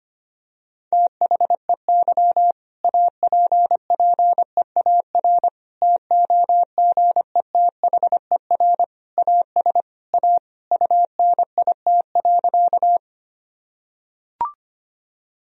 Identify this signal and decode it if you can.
Morse code, heard as a keyed tone, about 25 wpm, 700 Hz. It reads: THEY APPEAR TOGETHER AS A UNIT.
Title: Sentences from Top 1000 Words - Review Part 2 - 25wpm